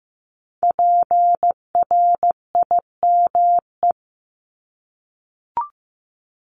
Morse code, heard as a keyed tone, about 15 wpm, 700 Hz. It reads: PRIME
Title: Multi-letter Phonemes 15wpm